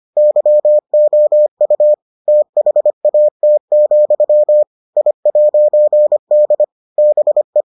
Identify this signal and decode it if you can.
Morse code, heard as a keyed tone, about 25 wpm, 600 Hz. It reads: YOU THAT, I'D BE